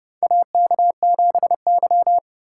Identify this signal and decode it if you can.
Morse code, heard as a keyed tone, about 31 wpm, 700 Hz. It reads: AK7Y
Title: USA Call Signs - 3x Character Spacing - 30wpm